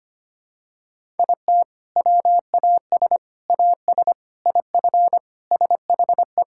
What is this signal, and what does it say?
Morse code, heard as a keyed tone, about 25 wpm, 700 Hz. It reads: IT WAS AS IF SHE
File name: practice / 25wpm / Sentences with 4-Word Phrases - 25wpm